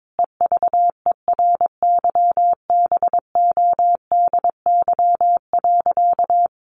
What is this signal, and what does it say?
Morse code, heard as a keyed tone, about 22 wpm, 700 Hz. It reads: EVERYBODY.
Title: Sentences from Top 1400 Words 22wpm